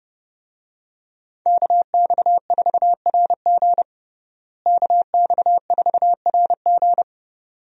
Morse code, heard as a keyed tone, about 30 wpm, 700 Hz. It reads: KX4RG KX4RG
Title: USA Call Signs - 5 Characters - Repeated 2x - 30wpm